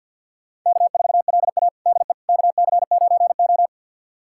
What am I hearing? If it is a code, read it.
Morse code, heard as a keyed tone, about 52 wpm, 700 Hz. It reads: K4LA DE KC9O
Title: USA Callsign DE Callsign - 4 Characters - 50wpm